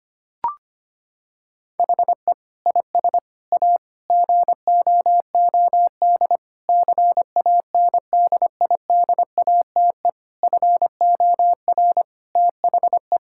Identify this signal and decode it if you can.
Morse code, heard as a keyed tone, about 25 wpm, 700 Hz. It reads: HE IS A GOOD CANDIDATE FOR THE
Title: Sentences from Top 1600 Words 25wpm